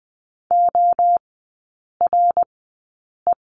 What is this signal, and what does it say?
Morse code, heard as a keyed tone, about 20 wpm, spaced as for 8 wpm, 700 Hz. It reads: ORE
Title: 053 Lesson 10e - Letter r - Three Characters 20wpm